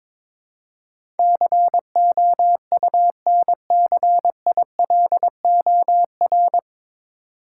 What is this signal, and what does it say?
Morse code, heard as a keyed tone, about 22 wpm, 700 Hz. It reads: COUNCILOR
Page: Morse Code Ninja practice 22wpm